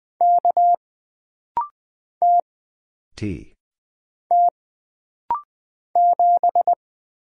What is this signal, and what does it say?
Morse code, heard as a keyed tone, about 20 wpm, 700 Hz. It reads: K T T 7
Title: Single Letter-Number Alt - Rapid-Fire 20wpm